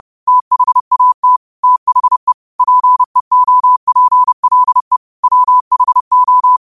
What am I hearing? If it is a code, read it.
Morse code, heard as a keyed tone, about 30 wpm, 1000 Hz. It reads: THAT THE PEOPLE WHO
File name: qotdfiles / 30wpm